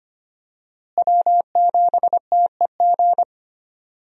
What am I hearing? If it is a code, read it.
Morse code, heard as a keyed tone, about 25 wpm, 700 Hz. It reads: W7TEG
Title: USA Call Signs - 5 Characters - Repeated 3x - 25wpm